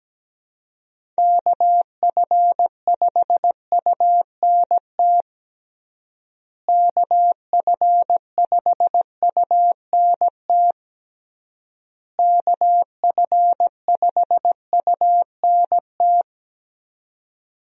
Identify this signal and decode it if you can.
Morse code, heard as a keyed tone, about 17 wpm, 700 Hz. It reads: KF5UNT KF5UNT KF5UNT
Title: USA Call Signs - 6 Characters - Repeated 3x - 17wpm